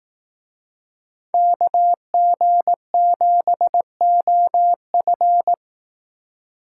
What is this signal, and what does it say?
Morse code, heard as a keyed tone, about 18 wpm, 700 Hz. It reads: KG7OF
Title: Speed Racing - USA 2x2 Call Signs 15wpm